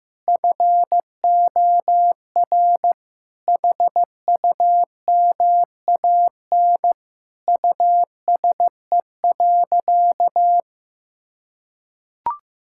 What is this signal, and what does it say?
Morse code, heard as a keyed tone, about 15 wpm, 700 Hz. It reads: FOR HUMAN USE.
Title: Sentences from Top 1000 Words - Review Part 6 - 15wpm